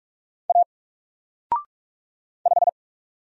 Morse code, heard as a keyed tone, about 44 wpm, 700 Hz. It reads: A 5
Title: Single Letter-Number 45wpm